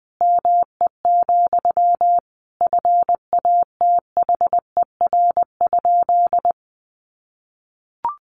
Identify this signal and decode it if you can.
Morse code, heard as a keyed tone, about 19 wpm, 700 Hz. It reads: ME, FATHER?
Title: Sentences from Top 700 Words 20wpm